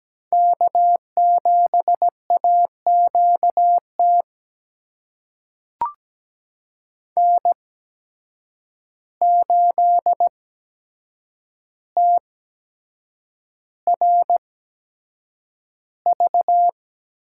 Morse code, heard as a keyed tone, about 17 wpm, spaced as for 7 wpm, 700 Hz. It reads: K7AQT N8TRV
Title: USA Call Signs - 5 Characters - 8x Character Spacing - 17wpm